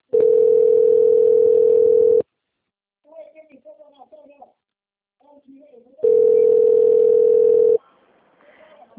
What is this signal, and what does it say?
Ringback tone (North America), 440 + 480 Hz, with speech.
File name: out- - - - - .4953